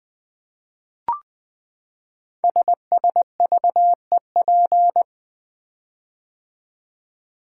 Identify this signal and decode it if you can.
Morse code, heard as a keyed tone, about 20 wpm, 700 Hz. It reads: SSVEP
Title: Code Groups - 5 Letters - 20wpm